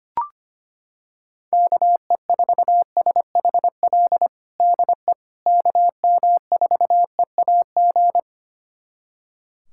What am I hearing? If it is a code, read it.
Morse code, heard as a keyed tone, about 25 wpm, 700 Hz. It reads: KE4SHL DE KM4EAG
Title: USA Callsign DE Callsign - 6 Characters - 25wpm